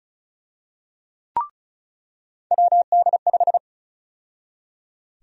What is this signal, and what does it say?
Morse code, heard as a keyed tone, about 36 wpm, 700 Hz. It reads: WD5
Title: USA Call Sign Prefixes - 35wpm